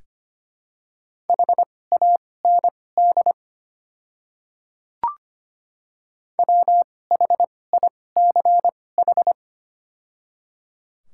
Morse code, heard as a keyed tone, about 25 wpm, spaced as for 18 wpm, 700 Hz. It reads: HAND WHICH
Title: 2x Character Spacing - Top 200 Words - 25wpm